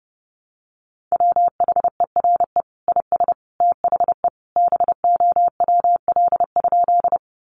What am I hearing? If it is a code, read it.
Morse code, heard as a keyed tone, about 30 wpm, 700 Hz. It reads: WHERE IS THE BOWL?
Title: Sentences from Top 1200 Words 30wpm